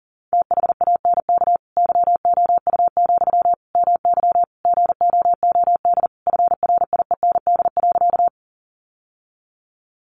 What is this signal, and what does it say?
Morse code, heard as a keyed tone, about 40 wpm, 700 Hz. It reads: THANK YOU, MY GOOD FRIEND.
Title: Sentences from Top 600 Words 40wpm